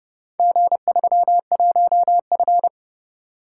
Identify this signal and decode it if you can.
Morse code, heard as a keyed tone, about 28 wpm, 700 Hz. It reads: G31F